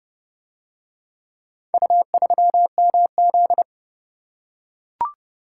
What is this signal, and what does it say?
Morse code, heard as a keyed tone, about 29 wpm, 700 Hz. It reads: U3MZ